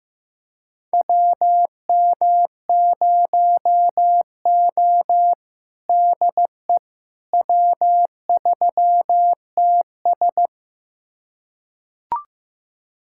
Morse code, heard as a keyed tone, about 15 wpm, 700 Hz. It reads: WM0O DE W3TS
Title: USA Callsign DE Callsign - 4 Characters - 15wpm